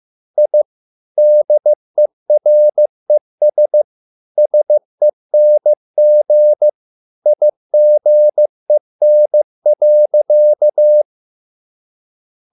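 Morse code, heard as a keyed tone, about 15 wpm, 600 Hz. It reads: I DERES SENG IGEN.
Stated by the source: Morse code at a slow 15 wpm from CW med Gnister.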